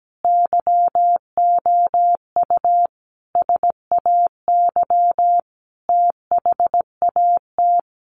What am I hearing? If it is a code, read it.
Morse code, heard as a keyed tone, about 17 wpm, 700 Hz. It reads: YOU SAY THAT
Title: Sentences from Top 200 Words 17wpm